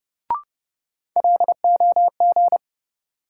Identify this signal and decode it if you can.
Morse code, heard as a keyed tone, about 30 wpm, 700 Hz. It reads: LOG
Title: Sentences from Top 1400 Words - New Word Drill - 30wpm